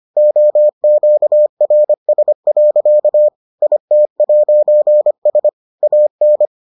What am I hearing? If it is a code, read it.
Morse code, heard as a keyed tone, about 25 wpm, 600 Hz. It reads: OQRS. IT'S AN